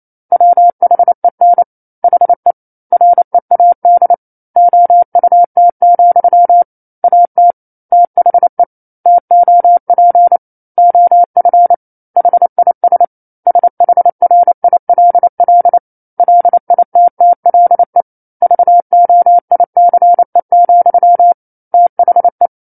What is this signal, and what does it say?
Morse code, heard as a keyed tone, about 28 wpm, 700 Hz. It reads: WHEN HE READ OUT, AT THE TOP OF HIS SHRILL LITTLE VOICE, THE
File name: 09-Chapter-09 28 WPM